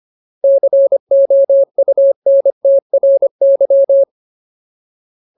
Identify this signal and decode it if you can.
Morse code, heard as a keyed tone, about 25 wpm, 550 Hz. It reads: COUNTRY
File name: Ultimate Word List / 25WPM Ultimate